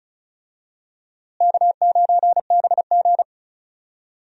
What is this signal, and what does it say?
Morse code, heard as a keyed tone, about 36 wpm, 700 Hz. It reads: K9BG
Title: USA Call Signs - 4 Characters - Repeated 2x - 35wpm